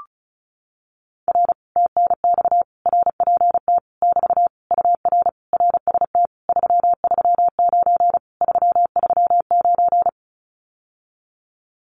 Morse code, heard as a keyed tone, about 36 wpm, 700 Hz. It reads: R TNX RPT <BT> UR RST 339 339
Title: Common QSO Phrases Encore 35wpm